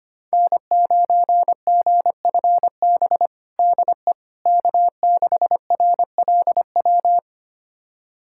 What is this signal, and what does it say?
Morse code, heard as a keyed tone, about 25 wpm, 700 Hz. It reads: N9GFB DE K6RLW
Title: USA Callsign DE Callsign - 5 Characters - 25wpm